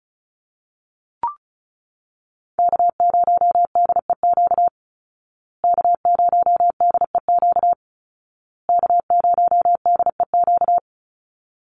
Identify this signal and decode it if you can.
Morse code, heard as a keyed tone, about 34 wpm, 700 Hz. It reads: K0DEQ K0DEQ K0DEQ
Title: Call Signs Repeated 2x - 35wpm